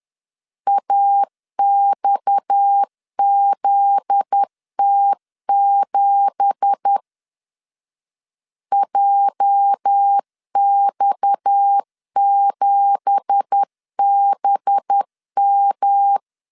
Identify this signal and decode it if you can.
Morse code, heard as a keyed tone, about 10 wpm, 800 Hz. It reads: AXZT7 JX7BM
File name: LEZIONI / 9 2867 10wpm 2733